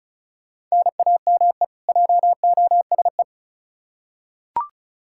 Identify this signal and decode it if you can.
Morse code, heard as a keyed tone, about 36 wpm, 700 Hz. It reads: NAME JOSE
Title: Practice QSO - Exchange 1 - Beginning - 35wpm